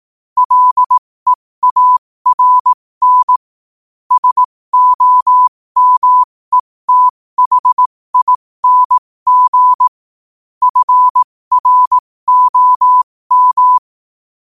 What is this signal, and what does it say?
Morse code, heard as a keyed tone, about 18 wpm, spaced as for 15 wpm, 1000 Hz. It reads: LEARN SOMETHING FROM